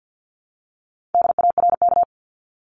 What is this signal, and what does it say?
Morse code, heard as a keyed tone, about 50 wpm, 700 Hz. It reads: DARK
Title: Top 500 Words 50wpm